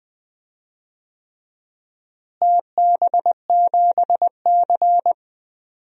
Morse code, heard as a keyed tone, about 21 wpm, 700 Hz. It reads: TB7C
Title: DXCC-100 Call Signs - 4 Characters - Repeated 3x - 20wpm